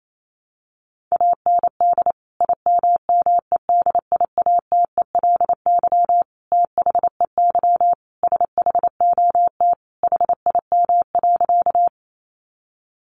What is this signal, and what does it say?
Morse code, heard as a keyed tone, about 28 wpm, 700 Hz. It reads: AND IMMEDIATELY THEY SHOT HIM.
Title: Sentences from Top 1100 Words 28wpm